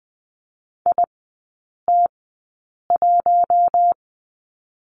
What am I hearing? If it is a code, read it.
Morse code, heard as a keyed tone, about 20 wpm, spaced as for 8 wpm, 700 Hz. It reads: IT1
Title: 114 Lesson 19e - Letter w - Three Characters 20wpm